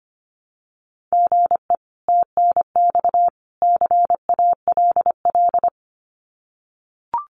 Morse code, heard as a keyed tone, about 25 wpm, 700 Hz. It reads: GE TNX CALL
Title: Common QSO Phrases 25wpm